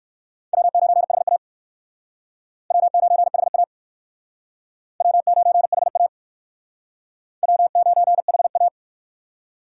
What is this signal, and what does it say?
Morse code, heard as a keyed tone, about 56 wpm, 700 Hz. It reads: W9HA W9HA W9HA W9HA
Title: Speed Racing - USA 1x2 Call Signs 45wpm